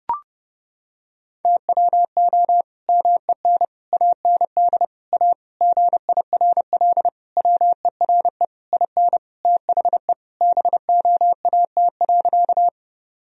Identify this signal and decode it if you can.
Morse code, heard as a keyed tone, about 30 wpm, 700 Hz. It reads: TWO MEN AND A GIRL WERE IN THE BOAT.
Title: Sentences from Top 1000 Words - Review Part 3 - 30wpm